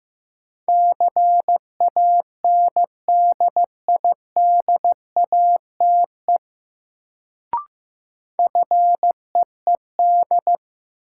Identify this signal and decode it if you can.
Morse code, heard as a keyed tone, about 15 wpm, 700 Hz. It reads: CANDIDATE FEED
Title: Sentences from Top 1600 Words - New Word Drill - 15wpm